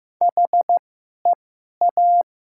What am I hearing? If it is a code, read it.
Morse code, heard as a keyed tone, about 15 wpm, spaced as for 11 wpm, 700 Hz. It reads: HEA
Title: 2x Character Spacing - Top 300 Words - 15wpm